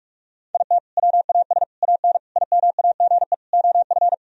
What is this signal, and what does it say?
Morse code, heard as a keyed tone, about 43 wpm, 700 Hz. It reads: IT WAS AN IMAGE OF
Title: Sentences from Top 1000 Words - Review Part 4 - 45wpm